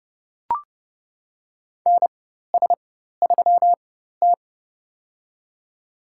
Morse code, heard as a keyed tone, about 30 wpm, spaced as for 14 wpm, 700 Hz. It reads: NS3T
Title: USA Call Signs - 4 Characters - 4x Character Spacing - 30wpm